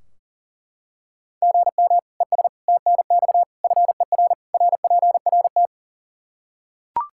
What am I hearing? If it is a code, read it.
Morse code, heard as a keyed tone, about 42 wpm, 700 Hz. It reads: GM ES TNX FER RPRT
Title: K1USN SST QSO Practice 40wpm